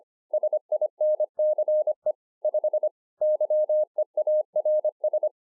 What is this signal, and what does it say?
Morse code, heard as a keyed tone, about 25 wpm, 610 Hz. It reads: SINCE 5 YEARS